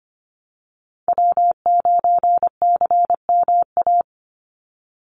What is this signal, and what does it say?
Morse code, heard as a keyed tone, about 25 wpm, 700 Hz. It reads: W9CMA
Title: USA Call Signs 25wpm